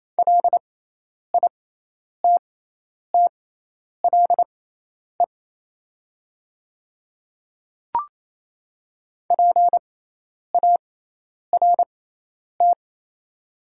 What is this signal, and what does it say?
Morse code, heard as a keyed tone, about 28 wpm, spaced as for 10 wpm, 690 Hz. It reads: LITTLE PART
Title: 6x Character Spacing - Top 200 Words - 28wpm